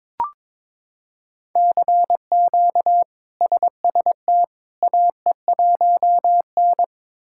Morse code, heard as a keyed tone, about 22 wpm, 700 Hz. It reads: CQ SST AE1N
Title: K1USN SST QSO Practice - 22wpm